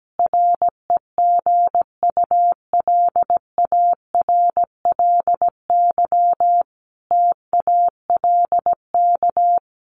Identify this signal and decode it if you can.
Morse code, heard as a keyed tone, about 17 wpm, 700 Hz. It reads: REGULARLY TALK
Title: Sentences from Top 1400 Words 17wpm